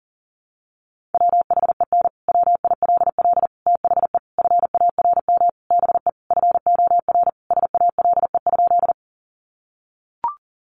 Morse code, heard as a keyed tone, about 40 wpm, 700 Hz. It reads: WHEN WILL THE FARM BE FOR SALE?
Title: Sentences from Top 1400 Words 40wpm